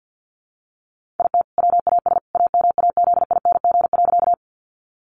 Morse code, heard as a keyed tone, about 50 wpm, 700 Hz. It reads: IT WAS AMAZING.
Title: Sentences from Top 1500 Words 50wpm